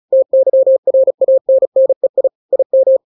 Morse code, heard as a keyed tone, about 36 wpm, 530 Hz. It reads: TYRANNEI IM
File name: daily / 35wpm